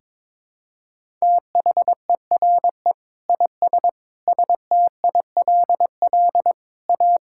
Morse code, heard as a keyed tone, about 22 wpm, 700 Hz. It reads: THERE IS STILL A
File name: Sets of 9 Words - 22wpm